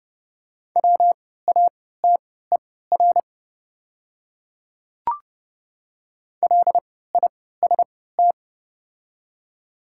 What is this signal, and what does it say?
Morse code, heard as a keyed tone, about 30 wpm, spaced as for 17 wpm, 690 Hz. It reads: WATER LIST
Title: 3x Character Spacing - Top 500 Words - 30wpm